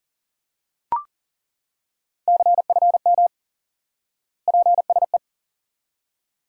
Morse code, heard as a keyed tone, about 38 wpm, 700 Hz. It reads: CFM PSE
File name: IQE Course 3 - Lesson 89 - YAGI - Two Elements - 40wpm